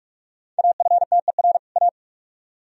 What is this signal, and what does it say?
Morse code, heard as a keyed tone, about 40 wpm, 700 Hz. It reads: AFTER A